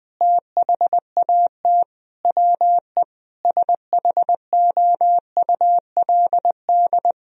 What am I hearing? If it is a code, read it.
Morse code, heard as a keyed tone, about 21 wpm, 700 Hz. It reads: THAT WE SHOULD
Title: Sentences from Top 1000 Words - Review Part 4 - 20wpm